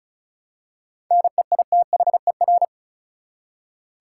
Morse code, heard as a keyed tone, about 34 wpm, 700 Hz. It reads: NEITHER
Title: Seven Letter Words - 35wpm